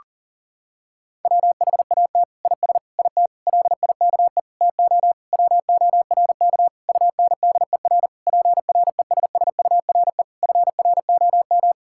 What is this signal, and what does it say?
Morse code, heard as a keyed tone, about 40 wpm, 700 Hz. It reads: WHAT IS IT LIKE TO WORK UNDER PRESSURE FROM
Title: Sentences from Top 1000 Words - Review Part 6 - 40wpm